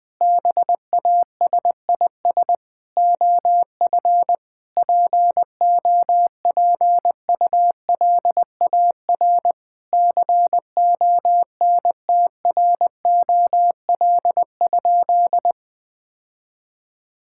Morse code, heard as a keyed tone, about 21 wpm, 700 Hz. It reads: BASIS OF POPULAR CONTROL?
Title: Sentences from Top 1000 Words - Review Part 5 - 20wpm